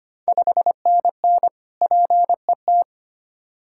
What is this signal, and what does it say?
Morse code, heard as a keyed tone, about 25 wpm, 690 Hz. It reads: 5NN PET
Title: Ontario QSO Party Exchange 25wpm